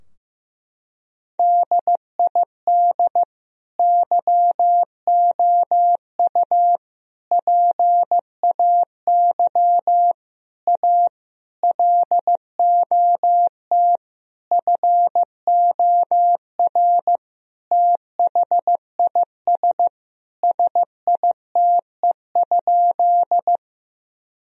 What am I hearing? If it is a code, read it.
Morse code, heard as a keyed tone, about 15 wpm, 700 Hz. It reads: DID YOU PAY A LOT FOR THIS SITE?
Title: Sentences from Top 1000 Words - Review Part 4 - 15wpm